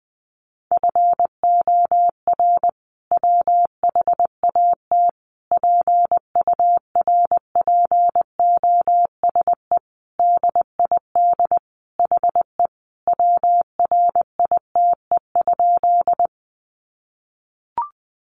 Morse code, heard as a keyed tone, about 19 wpm, 700 Hz. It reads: FOR WHAT PURPOSE DID HE WRITE?
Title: Sentences from Top 800 Words 20wpm